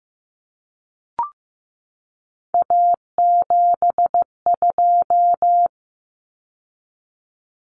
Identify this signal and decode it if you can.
Morse code, heard as a keyed tone, about 15 wpm, 700 Hz. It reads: A72
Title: Common Cut Numbers - 3-Digit 15wpm